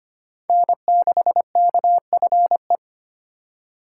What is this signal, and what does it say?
Morse code, heard as a keyed tone, about 25 wpm, 700 Hz. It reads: N6KFE
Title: USA Call Signs - 5 Characters - 3x Character Spacing - 25wpm